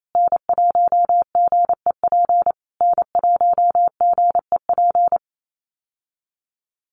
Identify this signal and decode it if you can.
Morse code, heard as a keyed tone, about 28 wpm, 700 Hz. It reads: N1GEP N1GEP
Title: USA Call Signs Repeated 3x 28wpm